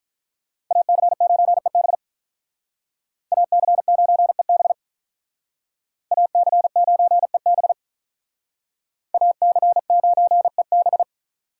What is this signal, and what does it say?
Morse code, heard as a keyed tone, about 43 wpm, 700 Hz. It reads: AC9EB AC9EB AC9EB AC9EB
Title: Speed Racing - USA Call Signs 35wpm